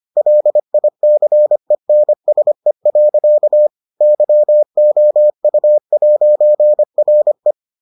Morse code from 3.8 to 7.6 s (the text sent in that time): YOU'RE